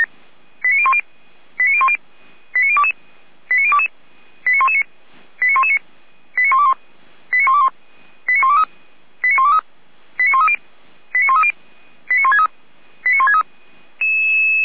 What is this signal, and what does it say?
Five-tone selective call (ZVEI), heard as quick runs of five tones, tones of about 70 ms. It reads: 89010 89010 89022 89022 89119 89119 89121 89121 89123 89123 89130 89130 89163 89163